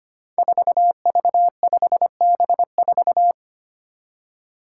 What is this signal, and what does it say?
Morse code, heard as a keyed tone, about 25 wpm, 700 Hz. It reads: 4V5B4
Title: Five Challenging Letter-Number Combinations 25wpm